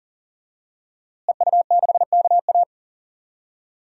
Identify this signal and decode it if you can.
Morse code, heard as a keyed tone, about 40 wpm, 700 Hz. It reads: EU6KA
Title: DXCC-100 Call Signs - 5 Characters - Repeated 2x - 40wpm